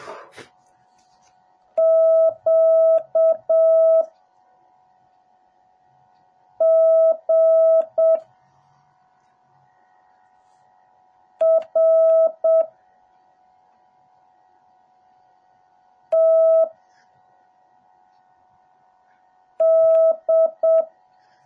Morse code, heard as a keyed tone, about 7 wpm, 650 Hz. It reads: Q G R T D
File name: jh8cbh / 2021cw / cw08a